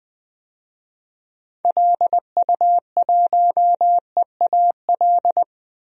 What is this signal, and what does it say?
Morse code, heard as a keyed tone, about 20 wpm, 700 Hz. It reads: LU1EAL
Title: DXCC-100 Call Signs - 6 Characters - 20wpm